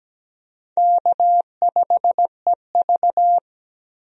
Morse code, heard as a keyed tone, about 17 wpm, 700 Hz. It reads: K5EV
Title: USA Call Signs - 4 Characters - 17wpm